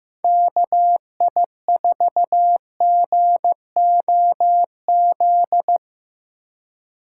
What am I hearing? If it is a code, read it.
Morse code, heard as a keyed tone, about 17 wpm, 700 Hz. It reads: KI4GOZ
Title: Speed Racing - USA 2x3 Call Signs 15wpm